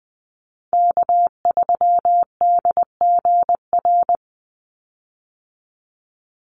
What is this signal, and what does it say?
Morse code, heard as a keyed tone, about 20 wpm, 700 Hz. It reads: K3DGR
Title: Call Signs - 20wpm